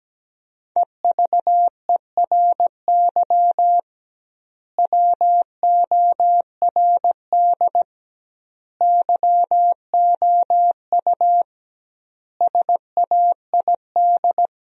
Morse code, heard as a keyed tone, about 17 wpm, 700 Hz. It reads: EVERY WORD YOU SAID
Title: 2x Word-Spacing - Sentences from Top 400 Words - 17wpm